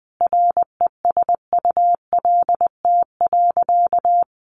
Morse code, heard as a keyed tone, about 20 wpm, 700 Hz. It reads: RESULT.
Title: Sentences from Top 700 Words 20wpm